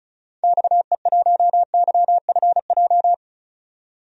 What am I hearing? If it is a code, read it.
Morse code, heard as a keyed tone, about 36 wpm, 700 Hz. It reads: XE1YFJ
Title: DXCC-100 Call Signs - 6 Characters - 35wpm